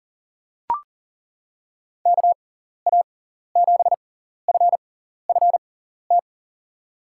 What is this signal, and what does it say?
Morse code, heard as a keyed tone, about 40 wpm, spaced as for 14 wpm, 700 Hz. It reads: KA7FFT
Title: USA Call Signs - 6 Characters - 6x Character Spacing - 40wpm